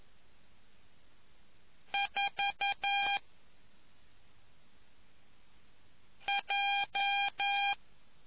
Morse code, heard as a keyed tone, about 11 wpm, spaced as for 3 wpm, 800 Hz. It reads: 4J